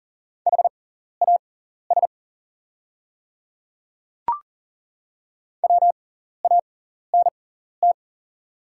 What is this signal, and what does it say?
Morse code, heard as a keyed tone, about 40 wpm, spaced as for 14 wpm, 690 Hz. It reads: HAS WANT
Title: 6x Character Spacing - Top 200 Words - 40wpm